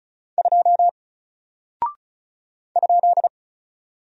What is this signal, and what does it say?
Morse code, heard as a keyed tone, about 36 wpm, 700 Hz. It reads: 2 ?
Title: Single Character (Letter:Number:Punctuation:Prosign) Alt 35wpm